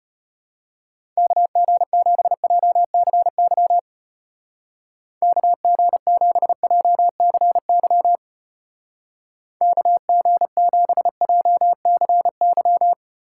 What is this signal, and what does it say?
Morse code, heard as a keyed tone, about 31 wpm, 700 Hz. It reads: KG7JCY KG7JCY KG7JCY
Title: Speed Racing - USA Call Signs 25wpm